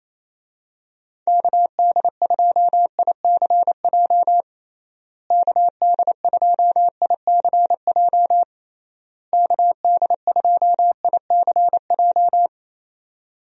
Morse code, heard as a keyed tone, about 28 wpm, 700 Hz. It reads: KD2ICJ KD2ICJ KD2ICJ
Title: USA Call Signs - 6 Characters - Repeated 3x - 28wpm